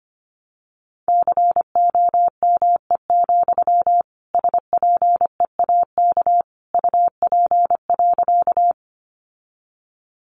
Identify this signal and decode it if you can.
Morse code, heard as a keyed tone, about 25 wpm, 700 Hz. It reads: COME, SPEAK UP.
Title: Sentences from Top 1100 Words 25wpm